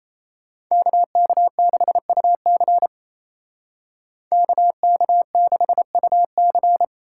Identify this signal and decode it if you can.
Morse code, heard as a keyed tone, about 36 wpm, 700 Hz. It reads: KK6UC KK6UC